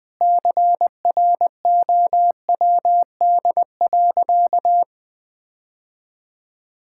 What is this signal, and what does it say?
Morse code, heard as a keyed tone, about 20 wpm, 700 Hz. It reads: CROWD.
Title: Sentences with 4-Word Phrases - Spoken Beginning - 20wpm